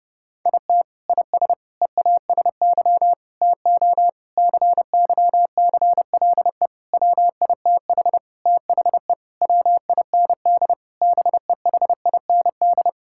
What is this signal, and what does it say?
Morse code, heard as a keyed tone, about 30 wpm, 700 Hz. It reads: IT IS EASY TO CYCLE WITH THE WIND BEHIND
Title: Sentences from Top 1000 Words - Review Part 6 - 30wpm